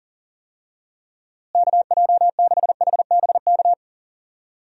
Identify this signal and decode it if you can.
Morse code, heard as a keyed tone, about 40 wpm, 700 Hz. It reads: KJ6HBK
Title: USA Call Signs - 6 Characters - Repeated 3x - 40wpm